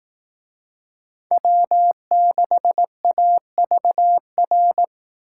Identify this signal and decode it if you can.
Morse code, heard as a keyed tone, about 18 wpm, 700 Hz. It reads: W6AVR
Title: Speed Racing - USA 1x3 Call Signs 15wpm